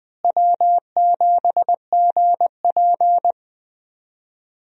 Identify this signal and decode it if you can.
Morse code, heard as a keyed tone, about 20 wpm, 700 Hz. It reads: W7GP
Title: USA Call Signs - 4 Characters - Repeated 3x - 20wpm